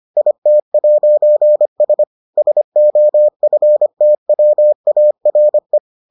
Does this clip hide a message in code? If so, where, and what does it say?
Yes: Morse code from 0.2 to 5.8 s, heard as a keyed tone, about 25 wpm, 600 Hz. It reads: IT'S SOFTWARE